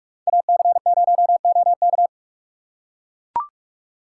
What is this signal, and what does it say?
Morse code, heard as a keyed tone, about 44 wpm, 700 Hz. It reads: AC0OK